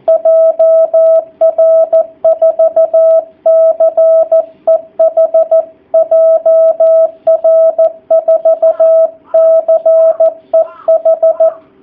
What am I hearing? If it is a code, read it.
Morse code, heard as a keyed tone, about 14 wpm, 660 Hz, with radio noise. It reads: JR4CEH JR4CEH
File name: jh8cbh / 2019cw / sp13- w